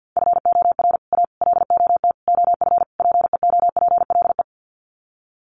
Morse code, heard as a keyed tone, about 50 wpm, 700 Hz. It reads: FOR A LOT OF PEOPLE
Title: Sets of 5 Words Encore 50wpm